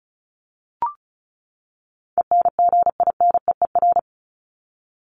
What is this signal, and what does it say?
Morse code, heard as a keyed tone, about 34 wpm, 700 Hz. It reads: ENGINEER
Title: ING Suffix - Top 100 Words with Base Words 35wpm